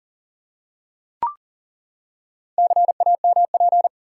Morse code, heard as a keyed tone, about 40 wpm, 700 Hz. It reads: CAMP
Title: Four Letter Words - 40wpm